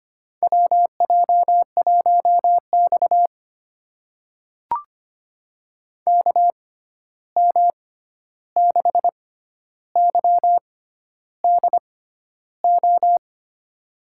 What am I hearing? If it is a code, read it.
Morse code, heard as a keyed tone, about 25 wpm, spaced as for 11 wpm, 700 Hz. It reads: WJ1X KM6YDO
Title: USA Call Signs - 6x Character Spacing - 25wpm